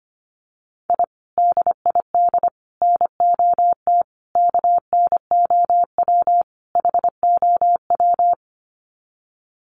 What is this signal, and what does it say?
Morse code, heard as a keyed tone, about 25 wpm, 700 Hz. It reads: I DID NOT KNOW HOW
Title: Sets of 5 Words Encore 25wpm